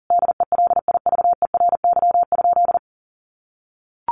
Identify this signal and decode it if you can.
Morse code, heard as a keyed tone, about 40 wpm, 700 Hz. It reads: DELIVERY?
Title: Sentences from Top 1500 Words 40wpm